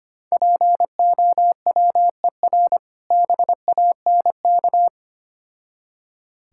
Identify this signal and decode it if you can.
Morse code, heard as a keyed tone, about 25 wpm, 700 Hz. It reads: POWER BANK
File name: practice / 25wpm / Difficult Words with Hints - 25wpm